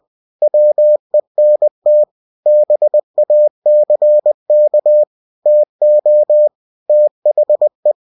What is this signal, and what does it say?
Morse code, heard as a keyed tone, about 20 wpm, 600 Hz. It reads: WENT BACK TO THE
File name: cw-books / winnie-the-pooh / winnie-20wpm-page-0085